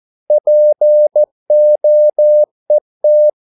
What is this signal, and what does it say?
Morse code, heard as a keyed tone, about 14 wpm, 600 Hz. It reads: POET
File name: Korte ord DK 14wpm